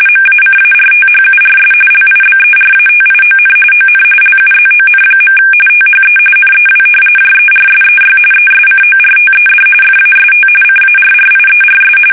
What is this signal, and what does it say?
RTTY, about 45.45 baud. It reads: MACARTHURS LIBERATION ON LUZON. THE NEW JERSEY ESCORTED FAST CARRIERS
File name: fsk850 / battleshipnj